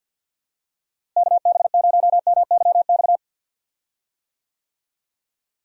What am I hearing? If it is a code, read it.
Morse code, heard as a keyed tone, about 48 wpm, 700 Hz. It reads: KB0MYX
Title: USA Call Signs - 6 Characters - Repeated 3x - 50wpm